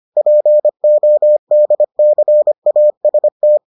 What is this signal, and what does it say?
Morse code, heard as a keyed tone, about 25 wpm, 600 Hz. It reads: PODCAST